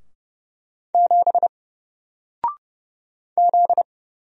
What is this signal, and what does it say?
Morse code, heard as a keyed tone, about 30 wpm, 700 Hz. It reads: 7 Z
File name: Single-Letter-Number-30wpm